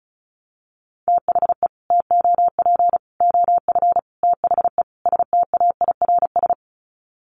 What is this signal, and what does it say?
Morse code, heard as a keyed tone, about 34 wpm, 700 Hz. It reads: THE TOP OF THE STAIRS
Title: Sets of 5 Words Encore 35wpm